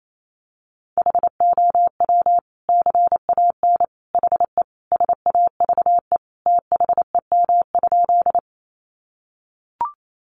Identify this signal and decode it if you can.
Morse code, heard as a keyed tone, about 28 wpm, 700 Hz. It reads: HOW CAN HE SAVE THEM?
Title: Sentences from Top 1100 Words 28wpm